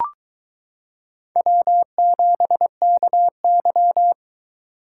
Morse code, heard as a keyed tone, about 22 wpm, 700 Hz. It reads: W7KY